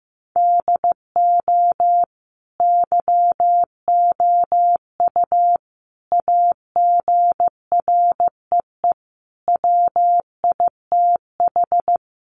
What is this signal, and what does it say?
Morse code, heard as a keyed tone, about 15 wpm, 700 Hz. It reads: DO YOU AGREE WITH
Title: Sets of 5 Words Encore 15wpm